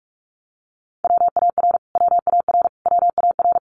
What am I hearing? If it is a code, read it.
Morse code, heard as a keyed tone, about 44 wpm, 700 Hz. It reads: WAR WAR WAR
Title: Top 500 Words Repeated 45wpm